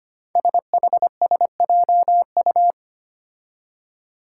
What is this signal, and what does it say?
Morse code, heard as a keyed tone, about 25 wpm, 700 Hz. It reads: SHSJU
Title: Five Challenging Letter-Number Combinations 25wpm